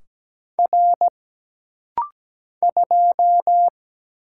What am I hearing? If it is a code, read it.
Morse code, heard as a keyed tone, about 17 wpm, 700 Hz. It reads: R 2
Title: Single Letter-Number Alt - Rapid-Fire 17wpm